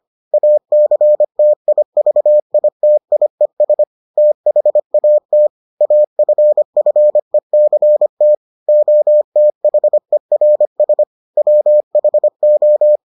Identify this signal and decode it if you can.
Morse code, heard as a keyed tone, about 25 wpm, 600 Hz. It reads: ACTIVITIES THAT AFFECT OTHERS WHO